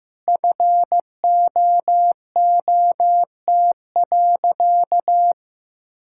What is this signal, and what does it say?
Morse code, heard as a keyed tone, about 15 wpm, 700 Hz. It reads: FOOT.
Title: Sentences from Top 1600 Words 15wpm